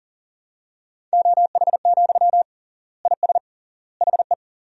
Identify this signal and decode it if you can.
Morse code, heard as a keyed tone, about 40 wpm, 700 Hz. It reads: OH, IS HE
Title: 3x Word-Spacing - Sentences from Top 500 Words - 40wpm